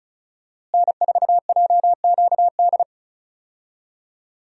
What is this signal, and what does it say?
Morse code, heard as a keyed tone, about 34 wpm, 700 Hz. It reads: N4JQD
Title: USA Call Signs - 5 Characters - Repeated 3x - 35wpm